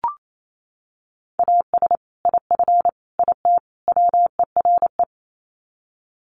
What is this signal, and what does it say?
Morse code, heard as a keyed tone, about 28 wpm, 700 Hz. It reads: AS IF IT WERE
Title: Sets of 4 Words 28wpm